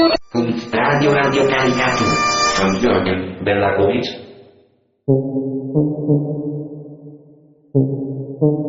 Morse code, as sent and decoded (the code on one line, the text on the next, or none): none